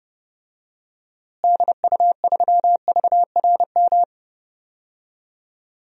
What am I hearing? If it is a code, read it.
Morse code, heard as a keyed tone, about 30 wpm, 700 Hz. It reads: DU3VRM
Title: DXCC-100 Call Signs - 6 Characters - Repeated 3x - 30wpm